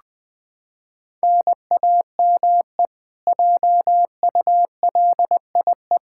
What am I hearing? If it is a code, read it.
Morse code, heard as a keyed tone, about 20 wpm, 700 Hz. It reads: NAME JULIE
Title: K1USN SST QSO Practice 20wpm